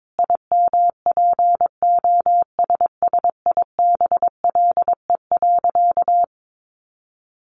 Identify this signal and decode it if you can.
Morse code, heard as a keyed tone, about 22 wpm, 700 Hz. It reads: IMPOSSIBLE.
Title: Sentences from Top 1100 Words 22wpm